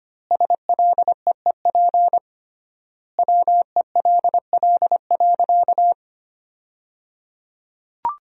Morse code, heard as a keyed tone, about 25 wpm, 700 Hz. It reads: SLEEP WELL.
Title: 3x Word-Spacing - Sentences from Top 500 Words - 25wpm